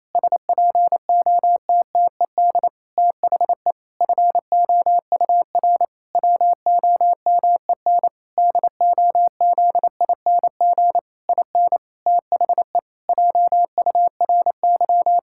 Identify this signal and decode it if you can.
Morse code, heard as a keyed tone, about 28 wpm, 700 Hz. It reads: SPOTTED THE FOUR WOMEN DOZING IN THE JURY
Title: Pangram Challenge v2 - 28wpm